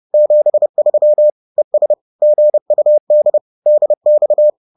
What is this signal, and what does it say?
Morse code, heard as a keyed tone, about 30 wpm, 600 Hz. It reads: 73 ES GUD DX